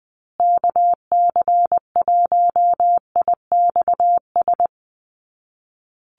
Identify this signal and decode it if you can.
Morse code, heard as a keyed tone, about 20 wpm, 700 Hz. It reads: KC1IXS
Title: USA Call Signs Repeated 3x 20wpm